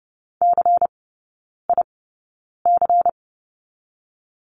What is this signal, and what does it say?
Morse code, heard as a keyed tone, about 30 wpm, 700 Hz. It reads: C I C
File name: Lesson 19e - Letter w - Three Characters 30wpm